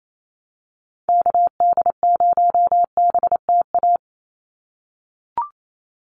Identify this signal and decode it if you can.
Morse code, heard as a keyed tone, about 28 wpm, 700 Hz. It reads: KD0BTA
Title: USA Call Signs 28wpm